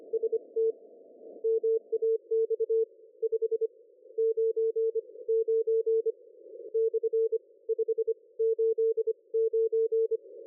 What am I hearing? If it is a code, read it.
Morse code, heard as a keyed tone, about 25 wpm, spaced as for 19 wpm, 450 Hz. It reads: ST MAX 599 / 589
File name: dlazaawansowanych / nagraniazpasma / ExpertSDR2 line out 23.01.2020 14.04.01